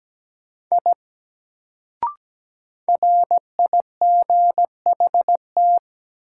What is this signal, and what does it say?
Morse code, heard as a keyed tone, about 17 wpm, 700 Hz. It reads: I RIGHT
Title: Sentences from Top 200 Words - New Word Drill - 17wpm